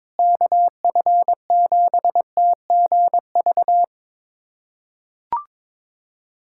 Morse code, heard as a keyed tone, about 22 wpm, 700 Hz. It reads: KF7TGV
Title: USA Callsign DE Callsign - 6 Characters - 22wpm